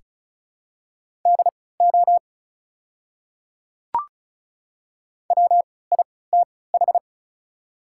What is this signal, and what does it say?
Morse code, heard as a keyed tone, about 36 wpm, spaced as for 20 wpm, 700 Hz. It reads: DO WITH
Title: 3x Character Spacing - Top 300 Words - 35wpm